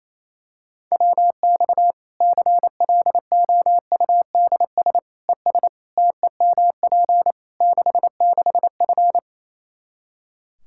Morse code, heard as a keyed tone, about 28 wpm, 700 Hz. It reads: WX CLOUDS ES TEMP 66F
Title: Practice QSO - Exchange 2 - Info - 28wpm